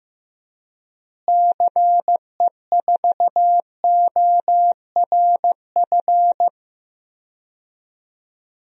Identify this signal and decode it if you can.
Morse code, heard as a keyed tone, about 15 wpm, 700 Hz. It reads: CE4ORF
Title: DXCC-100 Call Signs - 6 Characters - Repeated 3x - 15wpm